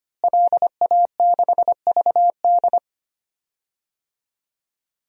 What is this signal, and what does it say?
Morse code, heard as a keyed tone, about 25 wpm, 700 Hz. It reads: LA6VD
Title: DXCC-100 Call Signs - 5 Characters - Repeated 3x - 25wpm